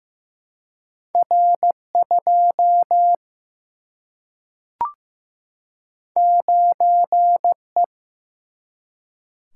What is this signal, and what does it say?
Morse code, heard as a keyed tone, about 15 wpm, 700 Hz. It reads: R 2 9 E
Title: Single-Letter Single-Number 15wpm